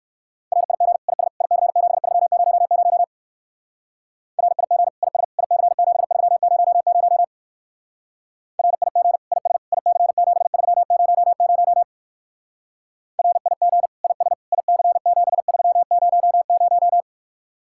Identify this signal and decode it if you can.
Morse code, heard as a keyed tone, about 51 wpm, 700 Hz. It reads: RIG IS IC7300 RIG IS IC7300 RIG IS IC7300 RIG IS IC7300
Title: Speed-Racing Common QSO Phrases - 45wpm